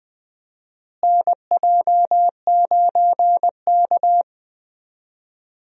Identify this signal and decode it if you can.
Morse code, heard as a keyed tone, about 20 wpm, 700 Hz. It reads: NJ9K
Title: USA Call Signs - 4 Characters - Repeated 2x - 20wpm